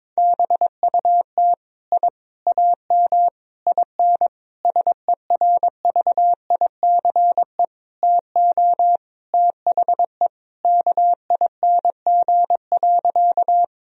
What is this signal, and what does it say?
Morse code, heard as a keyed tone, about 22 wpm, 700 Hz. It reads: BUT I AM IN SERVICE TO THE KING.
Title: Sentences from Top 1000 Words - Review Part 4 - 22wpm